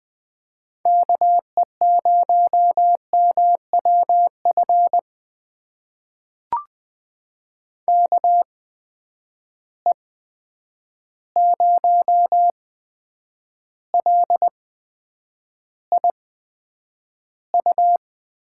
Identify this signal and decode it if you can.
Morse code, heard as a keyed tone, about 20 wpm, spaced as for 9 wpm, 700 Hz. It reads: KE0MWF KE0LIU
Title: USA Call Signs - 6 Characters - 8x Character Spacing - 20wpm